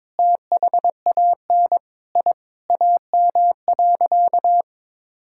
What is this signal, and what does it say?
Morse code, heard as a keyed tone, about 22 wpm, 700 Hz. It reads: THAN I AM.
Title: Sentences from Top 1000 Words - Review Part 5 - 22wpm